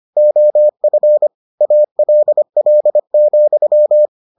Morse code, heard as a keyed tone, about 25 wpm, 600 Hz. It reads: OF ALL,